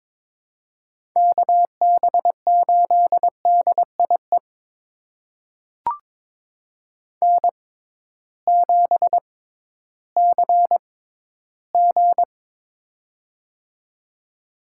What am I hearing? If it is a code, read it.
Morse code, heard as a keyed tone, about 22 wpm, spaced as for 13 wpm, 700 Hz. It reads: KB8DIE N7CG
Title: USA Call Signs - 6x Character Spacing - 22wpm